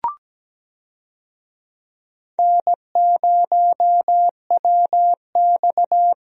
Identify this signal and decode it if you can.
Morse code, heard as a keyed tone, about 17 wpm, 700 Hz. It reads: N0WX
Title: USA Call Signs - 4 Characters - Repeated 3x - 17wpm